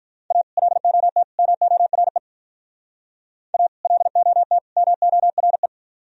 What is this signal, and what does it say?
Morse code, heard as a keyed tone, about 42 wpm, 700 Hz. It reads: A LOT MORE A LOT MORE
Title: Speed-Racing - Sets of 3 Words 35wpm